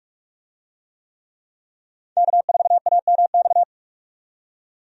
Morse code, heard as a keyed tone, about 46 wpm, 700 Hz. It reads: K4AMX
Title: USA Call Signs - 5 Characters - 45wpm